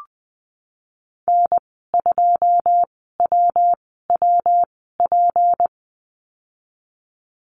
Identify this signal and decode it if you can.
Morse code, heard as a keyed tone, about 20 wpm, spaced as for 15 wpm, 700 Hz. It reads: N2WWP
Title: 167 Lesson 27g - Number 7 - Call Signs 20wpm